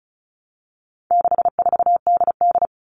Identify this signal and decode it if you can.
Morse code, heard as a keyed tone, about 34 wpm, 700 Hz. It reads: 64DD
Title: All Cut Numbers - 4-Digit 35wpm